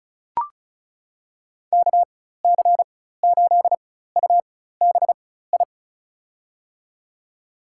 Morse code, heard as a keyed tone, about 34 wpm, spaced as for 16 wpm, 700 Hz. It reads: KC8UBI
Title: USA Call Signs - 6 Characters - 4x Character Spacing - 35wpm